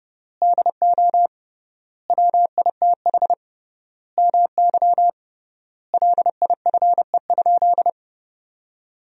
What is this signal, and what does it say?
Morse code, heard as a keyed tone, about 30 wpm, 700 Hz. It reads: DO WITH MY LIFE?
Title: 3x Word-Spacing - Sentences from Top 300 Words - 30wpm